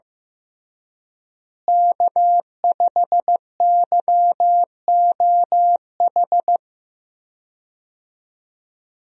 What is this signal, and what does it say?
Morse code, heard as a keyed tone, about 15 wpm, 700 Hz. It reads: K5YOH
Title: USA Call Signs - 5 Characters - Repeated 2x - 15wpm